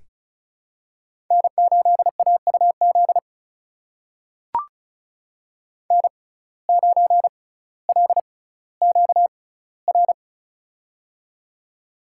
Morse code, heard as a keyed tone, about 35 wpm, spaced as for 18 wpm, 700 Hz. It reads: N8AUZ N9LQR
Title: USA Call Signs - 5 Characters - 6x Character Spacing - 35wpm